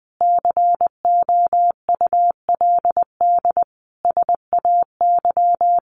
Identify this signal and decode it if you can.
Morse code, heard as a keyed tone, about 20 wpm, 700 Hz. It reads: COULD SAY
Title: Sentences from Top 1100 Words 20wpm